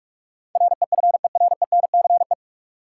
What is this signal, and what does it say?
Morse code, heard as a keyed tone, about 43 wpm, 700 Hz. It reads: REFERENCE
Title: Sentences from Top 1200 Words - New Word Drill Alt - 45wpm